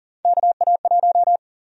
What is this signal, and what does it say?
Morse code, heard as a keyed tone, about 40 wpm, 700 Hz. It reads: KA1
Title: USA Call Sign Prefixes - 40wpm